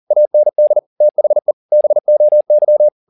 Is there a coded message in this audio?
Yes: Morse code from 0.1 to 2.9 s, heard as a keyed tone, about 40 wpm, 600 Hz. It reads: AND THE BOY